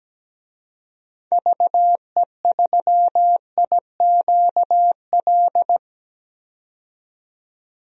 Morse code, heard as a keyed tone, about 17 wpm, 700 Hz. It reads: VE3IQL
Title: DXCC-100 Call Signs - 6 Characters - 17wpm